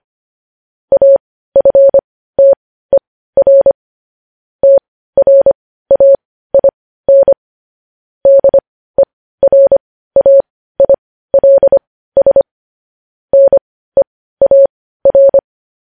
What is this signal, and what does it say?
Morse code, heard as a keyed tone, about 25 wpm, spaced as for 15 wpm, 550 Hz. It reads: AFTER TRAIN DERAILS NEAR